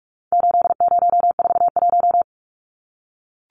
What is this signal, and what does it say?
Morse code, heard as a keyed tone, about 44 wpm, 700 Hz. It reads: 8041
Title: Numbers - 4-Digit 45wpm